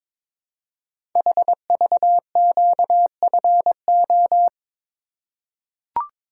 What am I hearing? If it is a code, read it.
Morse code, heard as a keyed tone, about 22 wpm, 700 Hz. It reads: HVQFO
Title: Code Groups - 5 Letters - 22wpm